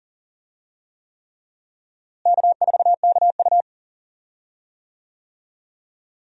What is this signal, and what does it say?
Morse code, heard as a keyed tone, about 40 wpm, 700 Hz. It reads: K4KU
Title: USA Call Signs - 4 Characters - Repeated 3x - 40wpm